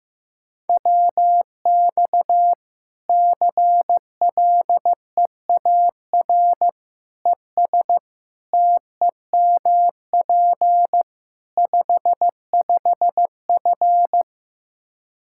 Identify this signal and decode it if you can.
Morse code, heard as a keyed tone, about 16 wpm, 700 Hz. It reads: WX CLEAR ES TEMP 55F
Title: Practice QSO - Exchange 2 - Info - 15wpm